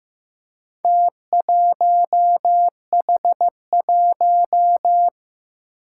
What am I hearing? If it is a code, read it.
Morse code, heard as a keyed tone, about 15 wpm, 700 Hz. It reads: T1H1
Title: Four Challenging Letter-Number Combinations 15wpm